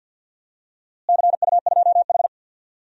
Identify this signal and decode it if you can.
Morse code, heard as a keyed tone, about 50 wpm, 700 Hz. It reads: CU2H
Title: DXCC-100 Call Signs - 4 Characters - Repeated 3x - 50wpm